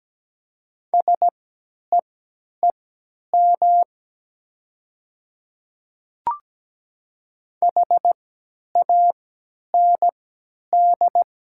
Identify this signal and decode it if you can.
Morse code, heard as a keyed tone, about 17 wpm, spaced as for 10 wpm, 700 Hz. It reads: SEEM HAND
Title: 3x Character Spacing - Top 300 Words - 17wpm